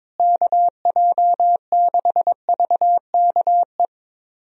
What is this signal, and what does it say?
Morse code, heard as a keyed tone, about 22 wpm, 700 Hz. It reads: KJ6VKE
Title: USA Call Signs - 6 Characters - Repeated 3x - 22wpm